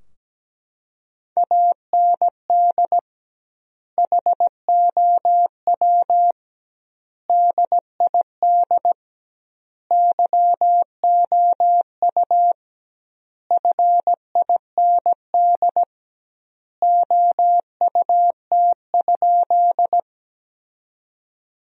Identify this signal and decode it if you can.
Morse code, heard as a keyed tone, about 17 wpm, 700 Hz. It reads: AND HOW DID YOU FIND OUT?
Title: 2x Word-Spacing - Sentences from Top 100 Words - 17wpm